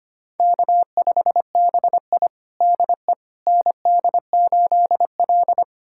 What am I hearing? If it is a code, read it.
Morse code, heard as a keyed tone, about 25 wpm, 700 Hz. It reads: K5BI DE ND8L
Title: USA Callsign DE Callsign - 4 Characters - 25wpm